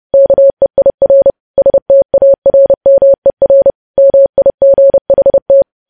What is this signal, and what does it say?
Morse code, heard as a keyed tone, about 30 wpm, 550 Hz. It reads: KEIR STARMER MIGHT
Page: news headlines in morse code at 30 wpm